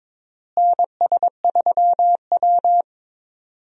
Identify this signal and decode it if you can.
Morse code, heard as a keyed tone, about 22 wpm, 700 Hz. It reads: NS3W
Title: USA Call Signs - 4 Characters - Repeated 3x - 22wpm